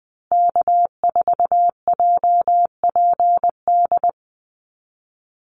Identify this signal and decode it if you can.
Morse code, heard as a keyed tone, about 20 wpm, 700 Hz. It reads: K4JPD
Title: Call Signs Repeated 2x - 20wpm